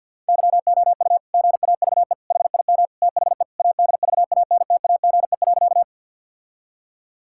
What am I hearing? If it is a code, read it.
Morse code, heard as a keyed tone, about 50 wpm, 700 Hz. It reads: YOU GAVE HIM THE ADVANTAGE.
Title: Sentences from Top 1000 Words - Review Part 6 - 50wpm